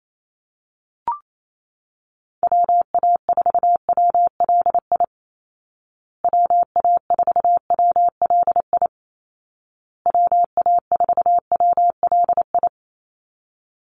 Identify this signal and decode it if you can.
Morse code, heard as a keyed tone, about 28 wpm, 700 Hz. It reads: WA4WLI WA4WLI WA4WLI
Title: Call Signs Repeated 2x - 28wpm